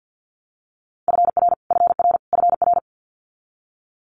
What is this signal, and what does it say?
Morse code, heard as a keyed tone, about 50 wpm, 700 Hz. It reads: FR FR FR
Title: Multi-letter Phonemes Repeated 3x 50wpm